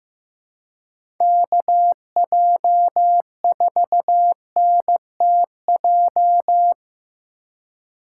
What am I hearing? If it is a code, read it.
Morse code, heard as a keyed tone, about 15 wpm, 700 Hz. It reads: KJ4NTJ
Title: USA Call Signs - 6 Characters - 4x Character Spacing - 15wpm